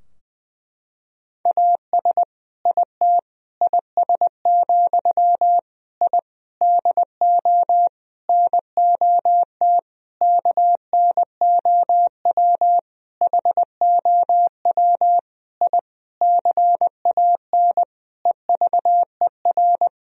Morse code, heard as a keyed tone, about 21 wpm, 700 Hz. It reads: AS IT IS, I DO NOT KNOW HOW I CAN EVER
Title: Sentences from Top 1000 Words - Review Part 2 - 20wpm